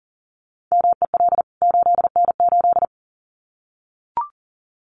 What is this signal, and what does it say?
Morse code, heard as a keyed tone, about 40 wpm, 700 Hz. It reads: MEL 8N8
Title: CWT Exchange 40wpm